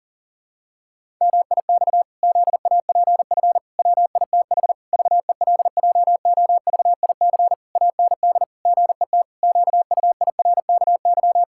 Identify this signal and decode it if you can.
Morse code, heard as a keyed tone, about 40 wpm, 700 Hz. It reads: MIX ZAPF WITH VELJOVIC AND GET QUIRKY
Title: Pangram Challenge v2 - 40wpm